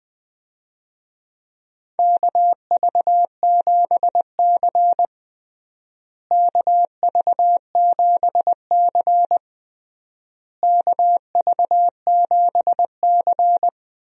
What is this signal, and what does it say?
Morse code, heard as a keyed tone, about 21 wpm, 700 Hz. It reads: KV7C KV7C KV7C
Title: USA Call Signs - 4 Characters - Repeated 3x - 20wpm